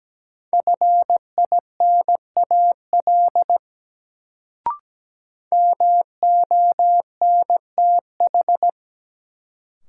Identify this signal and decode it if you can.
Morse code, heard as a keyed tone, about 17 wpm, 700 Hz. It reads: FINAL MONTH
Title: Sentences from Top 800 Words - New Word Drill - 17wpm